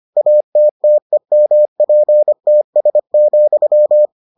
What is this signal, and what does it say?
Morse code, heard as a keyed tone, about 25 wpm, 600 Hz. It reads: ATTEMPTS,